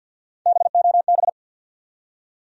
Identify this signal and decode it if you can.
Morse code, heard as a keyed tone, about 53 wpm, 700 Hz. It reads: BOB
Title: USA Common Names - Max 3 Letters - 50wpm